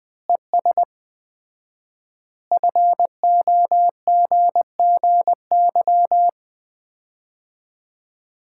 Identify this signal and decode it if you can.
Morse code, heard as a keyed tone, about 20 wpm, 700 Hz. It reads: ES FOGGY
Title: Common QSO Phrases - 4x Word-Spacing - 20wpm